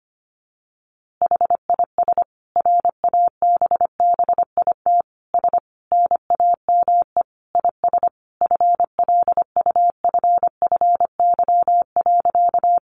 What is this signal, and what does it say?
Morse code, heard as a keyed tone, about 25 wpm, 700 Hz. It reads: HIS RABBIT S NAME IS FLUFFY.
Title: Easy Phrases 25wpm